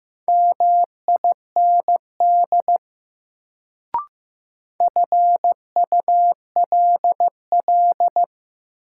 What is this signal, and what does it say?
Morse code, heard as a keyed tone, about 15 wpm, 700 Hz. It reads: MIND FULL
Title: Sentences from Top 500 Words - New Word Drill - 15wpm